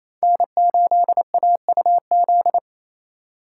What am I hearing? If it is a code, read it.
Morse code, heard as a keyed tone, about 28 wpm, 700 Hz. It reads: N8AUZ
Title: USA Call Signs - 5 Characters - 2x Character Spacing - 28wpm